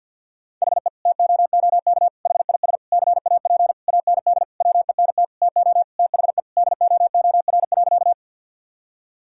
Morse code, heard as a keyed tone, about 50 wpm, 700 Hz. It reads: HE TOOK HIS CAP AND WENT TO THE DOOR.
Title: Sentences from Top 1600 Words 50wpm